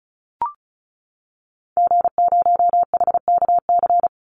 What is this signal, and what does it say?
Morse code, heard as a keyed tone, about 34 wpm, 700 Hz. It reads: G0HKC
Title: Call Signs Repeated 2x - 35wpm